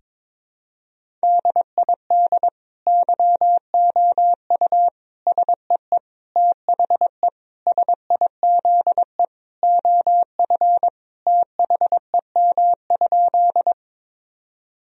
Morse code, heard as a keyed tone, about 22 wpm, 700 Hz. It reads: DID YOU SEE THE SIZE OF THEM?
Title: Sentences from Top 1000 Words - Review Part 2 - 22wpm